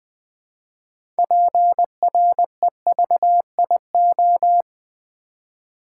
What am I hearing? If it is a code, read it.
Morse code, heard as a keyed tone, about 20 wpm, 700 Hz. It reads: PREVIO
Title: Build A Word - 2 Letters At A Time - 8-Letter Words - 20wpm